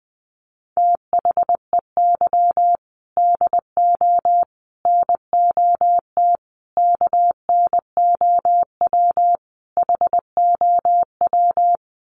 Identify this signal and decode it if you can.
Morse code, heard as a keyed tone, about 20 wpm, 700 Hz. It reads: THEY DO NOT KNOW HOW
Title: Sets of 5 Words Encore2 20wpm